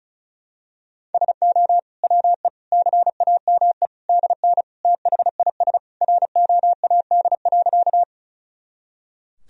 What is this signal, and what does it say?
Morse code, heard as a keyed tone, about 36 wpm, 700 Hz. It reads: SO WE CAME DN THIS ROAD.
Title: Sentences from Top 400 Words and Abbreviations 35wpm